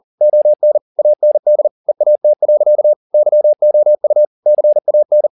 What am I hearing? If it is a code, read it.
Morse code, heard as a keyed tone, about 40 wpm, 600 Hz. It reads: ON AND EAT. YOU CAN